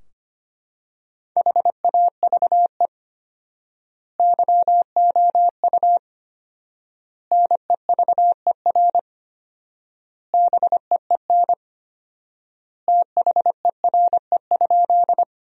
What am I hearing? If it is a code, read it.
Morse code, heard as a keyed tone, about 25 wpm, 700 Hz. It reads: HAVE YOU NEVER BEEN THERE?
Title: 4x Word-Spacing - Sentences from Top 300 Words - 25wpm